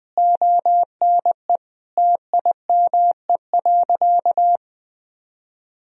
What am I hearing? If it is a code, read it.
Morse code, heard as a keyed tone, about 20 wpm, 700 Hz. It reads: ONE TIME.
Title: Sentences from Top 1000 Words - Review Part 2 - 20wpm